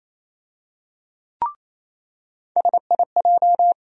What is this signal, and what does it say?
Morse code, heard as a keyed tone, about 28 wpm, 700 Hz. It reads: SIJ
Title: USA Call Sign Postfixes - 3 Letters 28wpm